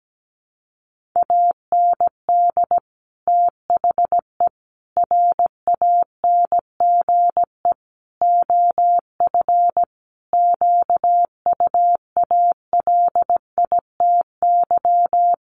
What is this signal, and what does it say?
Morse code, heard as a keyed tone, about 17 wpm, 700 Hz. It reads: AND THE RANGE OF QUALITY
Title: Sentences from Top 1500 Words 17wpm